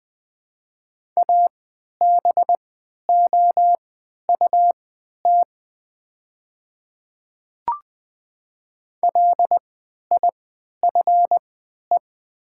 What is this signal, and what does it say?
Morse code, heard as a keyed tone, about 20 wpm, spaced as for 11 wpm, 700 Hz. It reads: ABOUT LIFE
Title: 3x Character Spacing - Top 500 Words - 20wpm